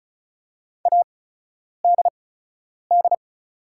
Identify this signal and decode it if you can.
Morse code, heard as a keyed tone, about 36 wpm, spaced as for 10 wpm, 700 Hz. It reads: ADD
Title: 8x Character Spacing - Top 500 Words - 35wpm